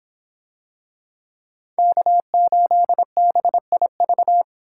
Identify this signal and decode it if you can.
Morse code, heard as a keyed tone, about 23 wpm, 700 Hz. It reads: K8BIV